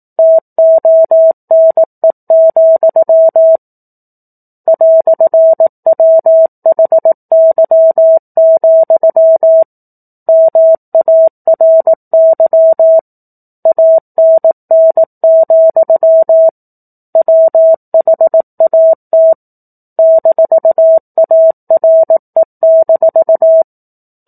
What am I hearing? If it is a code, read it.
Morse code, heard as a keyed tone, about 18 wpm, 650 Hz. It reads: TONE, "WHY, MARY ANN, WHAT -ARE-
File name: 04-Chapter-04 18 WPM